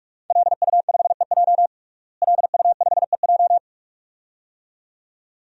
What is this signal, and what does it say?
Morse code, heard as a keyed tone, about 44 wpm, 700 Hz. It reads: LU5EJ LU5EJ
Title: DXCC-100 Call Signs - 5 Characters - Repeated 2x - 45wpm